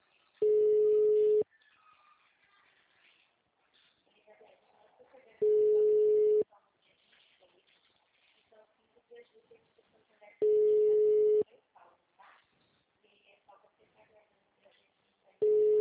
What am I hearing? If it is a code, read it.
Ringback tone (Europe, 425 Hz), with speech.